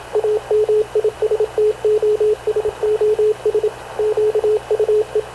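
Morse code, heard as a keyed tone, about 27 wpm, 430 Hz, with radio noise. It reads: AMISTOSOS QUE
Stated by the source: Havana Radio CLA press (PX) broadcast, recorded by Radio Officer Paul Dumesnil aboard mv CNAV QUEST/CZDO.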